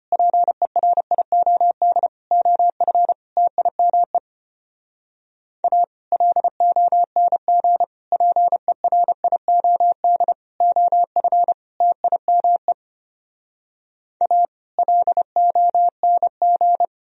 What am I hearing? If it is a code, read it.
Morse code, heard as a keyed tone, about 33 wpm, 700 Hz. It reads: PERIOD OF TIME A LONG PERIOD OF TIME A LONG